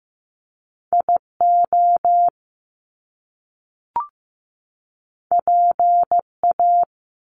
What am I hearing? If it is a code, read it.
Morse code, heard as a keyed tone, about 15 wpm, 700 Hz. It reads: IO PA
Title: Top Two Letter Combinations in Words 15wpm